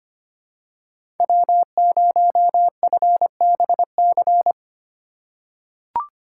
Morse code, heard as a keyed tone, about 25 wpm, 700 Hz. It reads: W0FBC
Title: USA Call Signs - 5 Characters - 2x Character Spacing - 25wpm